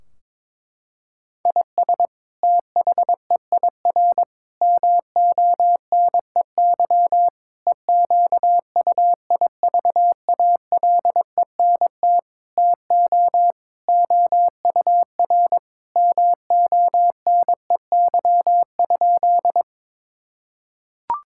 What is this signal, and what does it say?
Morse code, heard as a keyed tone, about 22 wpm, 700 Hz. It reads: IS THEIR MONEY EQUIVALENT TO OUR MONEY?
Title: Sentences from Top 1600 Words 22wpm